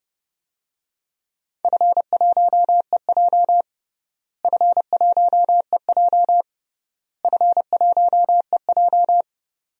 Morse code, heard as a keyed tone, about 29 wpm, 700 Hz. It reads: F1EJ F1EJ F1EJ
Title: DXCC-100 Call Signs - 4 Characters - Repeated 3x - 30wpm